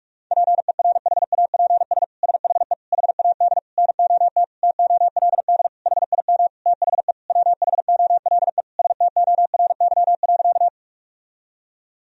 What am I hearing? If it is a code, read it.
Morse code, heard as a keyed tone, about 42 wpm, 700 Hz. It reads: PERHAPS SHE HAD NOT TOLD HIM THE WHOLE STORY.
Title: Sentences from Top 1000 Words - Review Part 1 - 45wpm